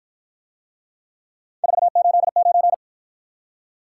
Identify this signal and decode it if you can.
Morse code, heard as a keyed tone, about 52 wpm, 700 Hz. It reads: 489